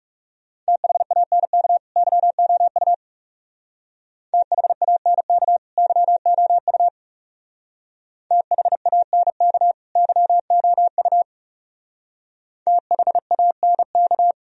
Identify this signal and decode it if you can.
Morse code, heard as a keyed tone, about 36 wpm, 700 Hz. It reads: THANK YOU THANK YOU THANK YOU THANK
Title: Speed-Racing - Sets of 2 Words 30wpm